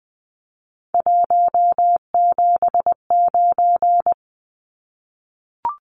Morse code, heard as a keyed tone, about 20 wpm, 700 Hz. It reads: 179
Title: Numbers - 3-Digit 20wpm